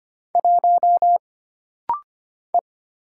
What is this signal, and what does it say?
Morse code, heard as a keyed tone, about 25 wpm, 700 Hz. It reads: 1 E
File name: Single Character (Letter:Number:Punctuation:Prosign) - Rapid-Fire - 25wpm